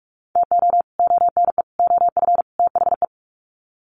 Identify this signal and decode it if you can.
Morse code, heard as a keyed tone, about 44 wpm, 700 Hz. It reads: TO ONE OF THE
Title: Sets of 4 Words 45wpm